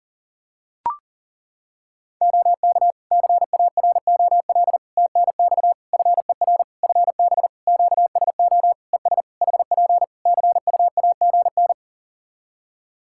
Morse code, heard as a keyed tone, about 40 wpm, 700 Hz. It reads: OK CAROL TNX FER FB QSO ES HP CUAGN
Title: Practice QSO - Exchange 4 - Ending - 40wpm